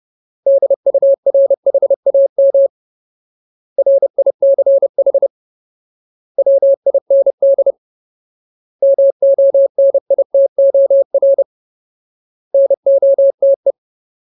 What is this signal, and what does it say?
Morse code, heard as a keyed tone, about 30 wpm, 550 Hz. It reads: DURHAM RICH WIND MONITOR NOTE
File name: Ultimate Word List / 30WPM Ultimate